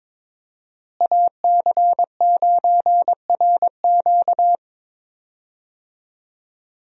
Morse code, heard as a keyed tone, about 21 wpm, 700 Hz. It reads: AC9RQ
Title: USA Call Signs - 5 Characters - Repeated 2x - 22wpm